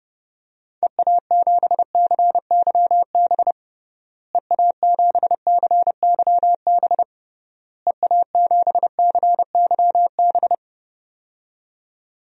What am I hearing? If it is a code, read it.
Morse code, heard as a keyed tone, about 30 wpm, 700 Hz. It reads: EA7CYB EA7CYB EA7CYB
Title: DXCC-100 Call Signs - 6 Characters - Repeated 3x - 30wpm